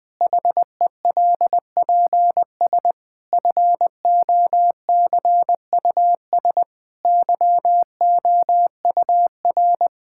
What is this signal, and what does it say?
Morse code, heard as a keyed tone, about 21 wpm, 700 Hz. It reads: HELPS FOCUS YOUR
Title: Difficult Sentences with Hints - 20wpm